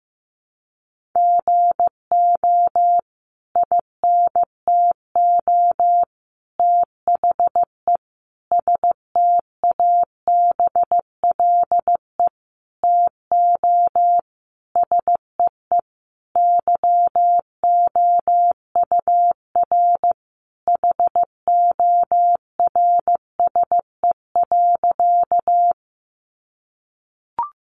Morse code, heard as a keyed tone, about 15 wpm, 700 Hz. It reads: GO INTO THE STABLE TO SEE YOUR HORSE.
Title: Sentences from Top 1500 Words 15wpm